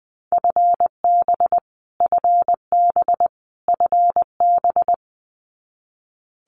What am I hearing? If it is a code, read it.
Morse code, heard as a keyed tone, about 20 wpm, 700 Hz. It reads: FB FB FB
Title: Abbreviations Repeated 3x - 20wpm